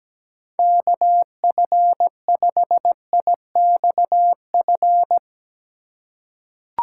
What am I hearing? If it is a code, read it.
Morse code, heard as a keyed tone, about 17 wpm, 700 Hz. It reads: KF5IXF
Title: USA Call Signs - 6 Characters - 3x Character Spacing - 17wpm